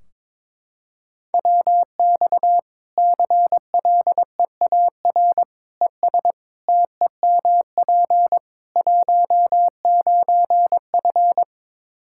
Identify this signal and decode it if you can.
Morse code, heard as a keyed tone, about 22 wpm, 700 Hz. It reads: WX CLEAR ES TEMP 19F
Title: Practice QSO - Exchange 2 - Info - 22wpm